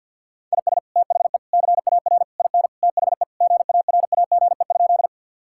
Morse code, heard as a keyed tone, about 50 wpm, 700 Hz. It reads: IS THE CAR IN THE GARAGE?
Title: Sentences from Top 1600 Words 50wpm